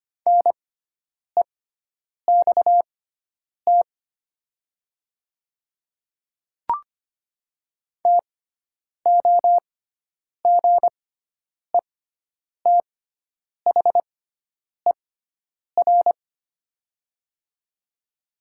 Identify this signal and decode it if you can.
Morse code, heard as a keyed tone, about 25 wpm, spaced as for 9 wpm, 700 Hz. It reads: NEXT TOGETHER
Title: 6x Character Spacing - Top 500 Words - 25wpm